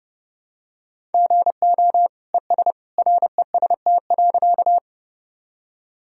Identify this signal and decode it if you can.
Morse code, heard as a keyed tone, about 30 wpm, 700 Hz. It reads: GO ES REST.
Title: Sentences from Top 500 Words and Abbreviations 30wpm